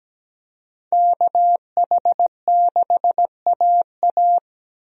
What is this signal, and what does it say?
Morse code, heard as a keyed tone, about 17 wpm, 700 Hz. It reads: KH6AA
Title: DXCC-100 Call Signs - 5 Characters - 17wpm